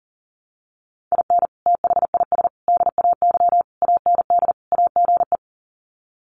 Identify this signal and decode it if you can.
Morse code, heard as a keyed tone, about 40 wpm, 700 Hz. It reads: IN THIS DAY AND AGE
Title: Sets of 5 Words Encore 40wpm